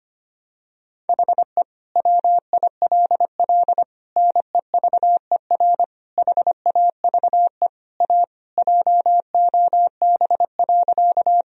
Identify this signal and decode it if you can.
Morse code, heard as a keyed tone, about 25 wpm, 700 Hz. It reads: HE WILL NEVER HAVE A JOB.
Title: Sentences from Top 1000 Words - Review Part 3 - 25wpm